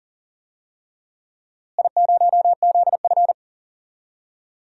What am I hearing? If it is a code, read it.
Morse code, heard as a keyed tone, about 38 wpm, 700 Hz. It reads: I0ZF
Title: DXCC-100 Call Signs - 4 Characters - Repeated 3x - 40wpm